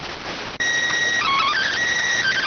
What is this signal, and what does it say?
SSTV (slow-scan television), PD 290.